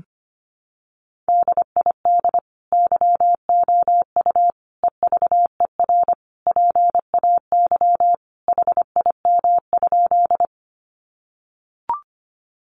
Morse code, heard as a keyed tone, about 25 wpm, 700 Hz. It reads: DID YOU EVER PAY HIM?
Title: Sentences from Top 700 Words 25wpm